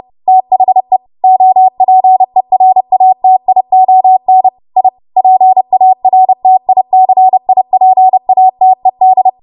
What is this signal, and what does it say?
Morse code, heard as a keyed tone, about 30 wpm, 750 Hz. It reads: THE OPERATION I PARTICIPATED
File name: Morse / 30WPM